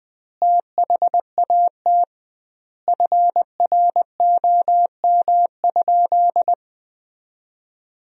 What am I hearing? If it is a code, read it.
Morse code, heard as a keyed tone, about 20 wpm, 700 Hz. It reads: THAT FROM?
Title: 2x Word-Spacing - Sentences from Top 300 Words - 20wpm